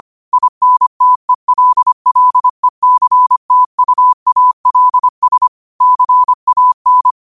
Morse code, heard as a keyed tone, about 25 wpm, 1000 Hz. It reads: INTELLECTUALS CAN
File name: qotdfiles / 25wpm